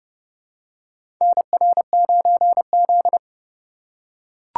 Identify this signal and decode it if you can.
Morse code, heard as a keyed tone, about 30 wpm, 700 Hz. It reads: NR9Z
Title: USA Call Signs - 4 Characters - 4x Character Spacing - 30wpm